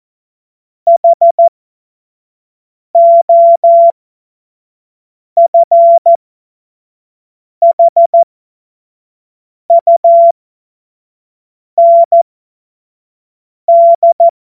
Morse code, heard as a keyed tone, about 14 wpm, spaced as for 5 wpm, 680 Hz. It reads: HOFHUND
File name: Witz-12-August-2025-25-F70